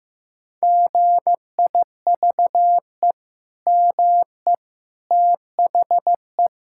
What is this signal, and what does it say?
Morse code, heard as a keyed tone, about 15 wpm, 700 Hz. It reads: GIVE ME THE
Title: Sentences from Top 1600 Words 15wpm